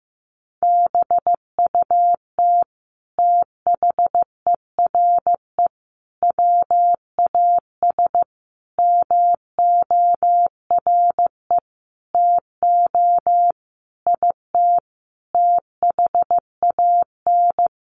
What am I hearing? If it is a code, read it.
Morse code, heard as a keyed tone, about 15 wpm, 700 Hz. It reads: BUT THERE WAS MORE TO IT THAN
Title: Sentences from Top 100 Words 15wpm